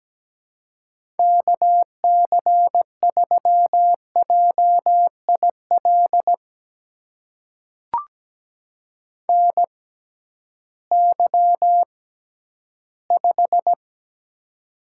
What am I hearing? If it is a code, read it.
Morse code, heard as a keyed tone, about 17 wpm, spaced as for 11 wpm, 700 Hz. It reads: KC3JIL NY5
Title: USA Call Signs - 6x Character Spacing - 17wpm